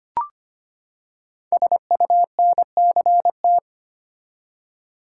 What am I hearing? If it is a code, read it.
Morse code, heard as a keyed tone, about 25 wpm, 700 Hz. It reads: SUNCT
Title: Code Groups - 5 Letter:Numbers - 25wpm